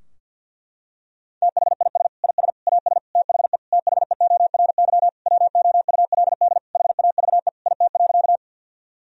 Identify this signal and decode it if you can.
Morse code, heard as a keyed tone, about 51 wpm, 700 Hz. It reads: THIS IS AS THE THEORY WOULD HAVE IT.
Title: Sentences from Top 1000 Words - Review Part 3 - 50wpm